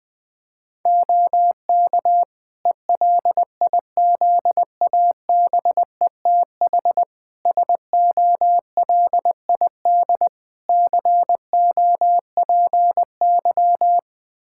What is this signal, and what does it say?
Morse code, heard as a keyed tone, about 20 wpm, 700 Hz. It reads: OK ELIZABETH SOLID COPY
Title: Practice QSO - Exchange 3 - More Info - 20wpm